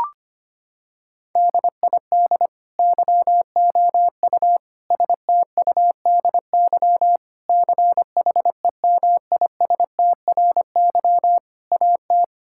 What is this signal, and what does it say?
Morse code, heard as a keyed tone, about 25 wpm, 700 Hz. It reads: DID YOU STUDY CHEMISTRY AT
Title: Sentences from Top 1000 Words - Review Part 6 - 25wpm